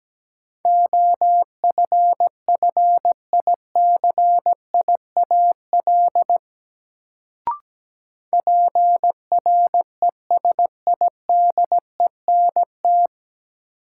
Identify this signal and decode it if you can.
Morse code, heard as a keyed tone, about 17 wpm, 700 Hz. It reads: OFFICIAL PRESIDENT
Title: Sentences from Top 1000 Words - New Word Drill - 17wpm